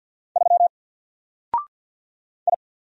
Morse code, heard as a keyed tone, about 50 wpm, 700 Hz. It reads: 3 I
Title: Single Letter-Number Alt 50wpm